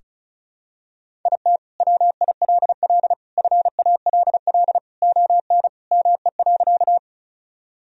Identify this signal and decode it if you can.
Morse code, heard as a keyed tone, about 34 wpm, 700 Hz. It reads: IT WILL FALL ON ME.
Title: Sentences from Top 1000 Words - Review Part 2 - 35wpm